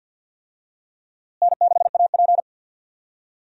Morse code, heard as a keyed tone, about 47 wpm, 700 Hz. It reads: N6AP
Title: USA Call Signs - 4 Characters - Repeated 2x - 50wpm